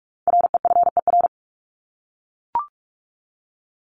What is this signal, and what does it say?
Morse code, heard as a keyed tone, about 44 wpm, 700 Hz. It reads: REFER
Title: ED Suffix - Top 100 Words with Base Words 45wpm